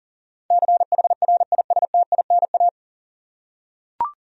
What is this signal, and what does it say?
Morse code, heard as a keyed tone, about 40 wpm, 700 Hz. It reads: CHRISTINA
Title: USA Common Names - 40wpm